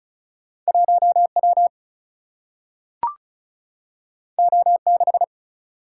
Morse code, heard as a keyed tone, about 36 wpm, 700 Hz. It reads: 1 W O 6
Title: Single-Letter Single-Number 35wpm